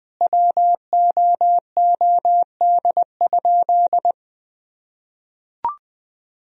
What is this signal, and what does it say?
Morse code, heard as a keyed tone, about 20 wpm, 700 Hz. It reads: WOOD?
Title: Sentences from Top 1000 Words - Review Part 2 - 20wpm